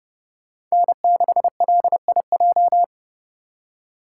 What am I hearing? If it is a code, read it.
Morse code, heard as a keyed tone, about 38 wpm, 700 Hz. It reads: N6LIJ